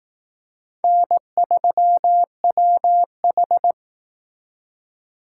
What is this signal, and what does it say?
Morse code, heard as a keyed tone, about 19 wpm, 700 Hz. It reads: N3WH